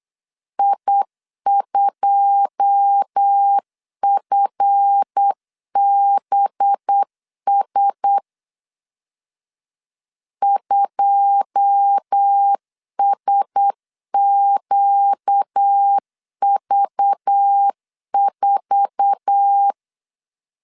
Morse code, heard as a keyed tone, about 8 wpm, 800 Hz. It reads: I2FBS 2SQV4
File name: CW / DET / 1 9 1500C S C 8wpm 5653